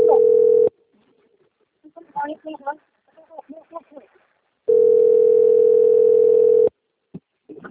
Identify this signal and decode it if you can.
Ringback tone (North America), 440 + 480 Hz, with speech.
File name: out- - - - - .13288